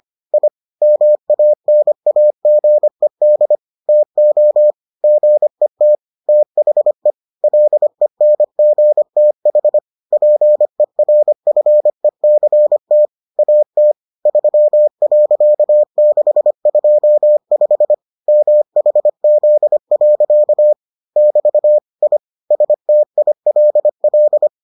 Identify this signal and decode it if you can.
Morse code, heard as a keyed tone, about 25 wpm, 600 Hz. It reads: I MANAGED TO GET THE LENGTH PERFECT AT 3.625 MHZ. <BT> I STILL